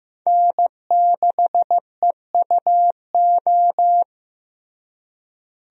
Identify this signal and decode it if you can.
Morse code, heard as a keyed tone, about 15 wpm, 700 Hz. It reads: N6EUO
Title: USA Call Signs - 5 Characters - Repeated 3x - 15wpm